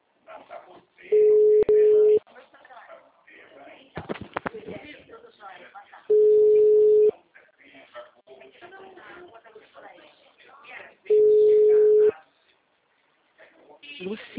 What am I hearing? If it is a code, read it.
Ringback tone (Europe, 425 Hz), with speech.
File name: out- -2029